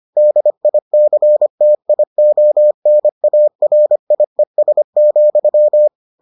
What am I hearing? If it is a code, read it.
Morse code, heard as a keyed tone, about 25 wpm, 600 Hz. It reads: DICTIONARIES,